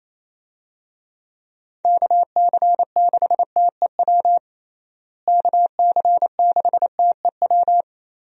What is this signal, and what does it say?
Morse code, heard as a keyed tone, about 28 wpm, 700 Hz. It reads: KC6TEW KC6TEW
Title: USA Call Signs - 6 Characters - Repeated 3x - 28wpm